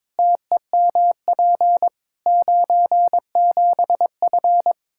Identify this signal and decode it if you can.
Morse code, heard as a keyed tone, about 22 wpm, 700 Hz. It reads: TEMP 97F
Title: Practice QSO - Exchange 2 - Info - 22wpm